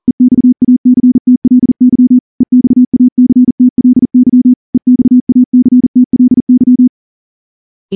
Morse code, heard as a keyed tone, about 38 wpm, 250 Hz. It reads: EXACTLY EXACTLY EXACTLY
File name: morse-training-40WPM